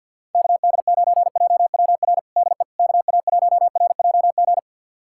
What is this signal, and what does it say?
Morse code, heard as a keyed tone, about 46 wpm, 700 Hz. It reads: KD9JWR DE KA1RJG
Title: USA Callsign DE Callsign 50wpm